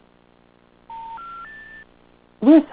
Special information tone.